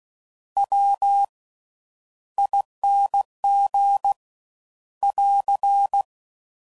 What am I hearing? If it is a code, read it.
Morse code, heard as a keyed tone, about 16 wpm, spaced as for 10 wpm, 800 Hz. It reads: WING <AR>
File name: Suffixes / Special INGs / Slow With Faster Ings